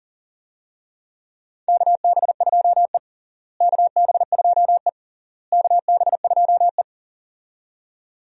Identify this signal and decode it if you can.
Morse code, heard as a keyed tone, about 40 wpm, 700 Hz. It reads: KB2E KB2E KB2E
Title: USA Call Signs - 4 Characters - Repeated 3x - 40wpm